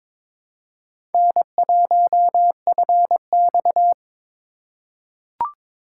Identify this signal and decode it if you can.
Morse code, heard as a keyed tone, about 23 wpm, 700 Hz. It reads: N1FX
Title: USA Call Signs - 4 Characters - 2x Character Spacing - 22wpm